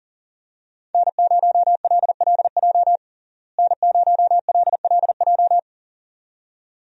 Morse code, heard as a keyed tone, about 40 wpm, 700 Hz. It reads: N0LLJ N0LLJ
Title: USA Call Signs - 5 Characters - Repeated 2x - 40wpm